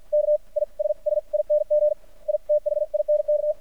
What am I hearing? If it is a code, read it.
Morse code, heard as a keyed tone, about 53 wpm, spaced as for 41 wpm, 600 Hz. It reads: KISSING I THINK
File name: archived / CRSnet